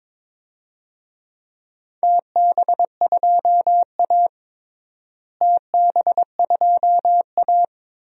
Morse code, heard as a keyed tone, about 22 wpm, 700 Hz. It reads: TB2A TB2A
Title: DXCC-100 Call Signs - 4 Characters - Repeated 3x - 22wpm